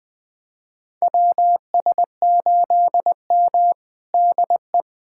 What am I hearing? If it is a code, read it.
Morse code, heard as a keyed tone, about 20 wpm, 700 Hz. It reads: WS8M DE
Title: USA Callsign DE Callsign - 4 Characters - 20wpm